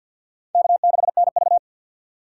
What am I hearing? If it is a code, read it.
Morse code, heard as a keyed tone, about 50 wpm, 700 Hz. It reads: K6NV
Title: USA Call Signs - 4 Characters - 2x Character Spacing - 50wpm